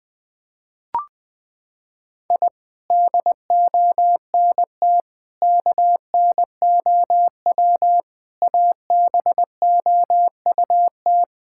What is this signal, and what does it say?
Morse code, heard as a keyed tone, about 20 wpm, 700 Hz. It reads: I DONT KNOW ABOUT
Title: Sentences with 6-Word Phrases - 20wpm